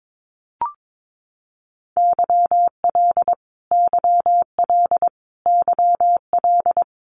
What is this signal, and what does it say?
Morse code, heard as a keyed tone, about 22 wpm, 700 Hz. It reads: YL YL YL
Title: Abbreviations Repeated 3x - 22wpm